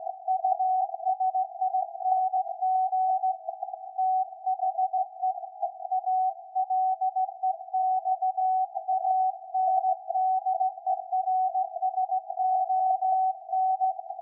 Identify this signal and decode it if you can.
Morse code, heard as a keyed tone, about 16 wpm, 750 Hz. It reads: USING THE ALEXANDERSON
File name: iz7slz / VLF / SAQ 13-02-2023